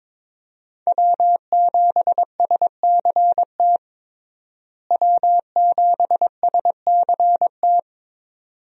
Morse code, heard as a keyed tone, about 22 wpm, 700 Hz. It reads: W7SCT W7SCT
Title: USA Call Signs - 5 Characters - Repeated 2x - 22wpm